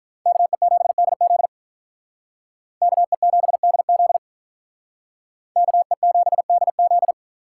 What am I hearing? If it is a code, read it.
Morse code, heard as a keyed tone, about 43 wpm, 700 Hz. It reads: KE7DZ KE7DZ KE7DZ
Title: Speed Racing - USA 2x2 Call Signs 35wpm